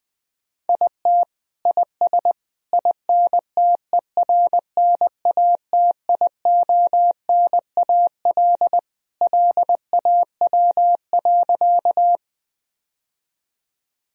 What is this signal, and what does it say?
Morse code, heard as a keyed tone, about 19 wpm, 700 Hz. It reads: IT IS INTERNATIONAL LAW.
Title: Sentences from Top 1000 Words - Review Part 4 - 20wpm